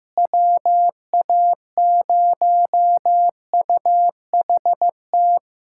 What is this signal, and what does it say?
Morse code, heard as a keyed tone, about 15 wpm, 700 Hz. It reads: WA0UHT
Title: USA Callsign DE Callsign - 6 Characters - 15wpm